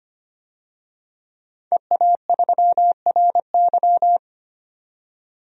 Morse code, heard as a keyed tone, about 25 wpm, 700 Hz. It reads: EA3RY
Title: DXCC-100 Call Signs - 5 Characters - 25wpm